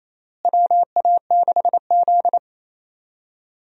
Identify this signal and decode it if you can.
Morse code, heard as a keyed tone, about 27 wpm, 700 Hz. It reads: WA6Z
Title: USA Call Signs - 4 Characters - Repeated 3x - 28wpm